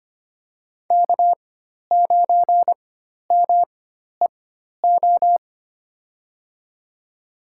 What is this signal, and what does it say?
Morse code, heard as a keyed tone, about 25 wpm, spaced as for 12 wpm, 700 Hz. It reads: K9MEO
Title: USA Call Signs - 4x Character Spacing - 25wpm